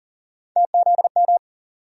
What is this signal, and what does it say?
Morse code, heard as a keyed tone, about 40 wpm, 700 Hz. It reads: TZM
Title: USA Call Sign Postfixes - 3 Letters 40wpm